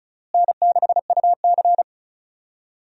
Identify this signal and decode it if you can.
Morse code, heard as a keyed tone, about 36 wpm, 700 Hz. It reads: N6UC
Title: USA Call Signs - 4 Characters - Repeated 3x - 35wpm